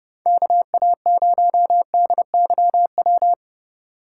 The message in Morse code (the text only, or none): KA0DYW